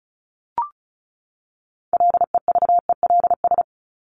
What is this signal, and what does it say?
Morse code, heard as a keyed tone, about 34 wpm, 700 Hz. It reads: LEVELS
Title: ES:S Suffix - Top 100 Words with Base Words 35wpm